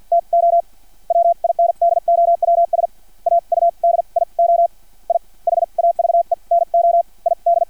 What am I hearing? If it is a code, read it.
Morse code, heard as a keyed tone, about 50 wpm, spaced as for 41 wpm, 680 Hz. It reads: TO WINDOWS AUDIO I HAVE NO ID
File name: archived / CRSnet